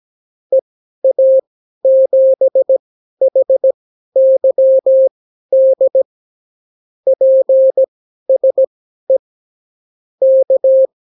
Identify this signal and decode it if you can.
Morse code, heard as a keyed tone, about 17 wpm, spaced as for 12 wpm, 530 Hz. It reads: EA7HYD PSE K
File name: CW QSO libroCW4 18-8ppm